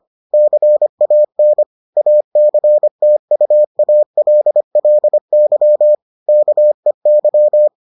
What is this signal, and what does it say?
Morse code, heard as a keyed tone, about 25 wpm, 600 Hz. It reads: CAN ACTUALLY KEY